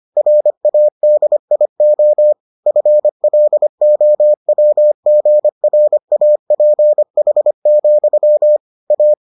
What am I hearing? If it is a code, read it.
Morse code, heard as a keyed tone, about 25 wpm, 600 Hz. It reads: RADIO FLOWGRAPH, A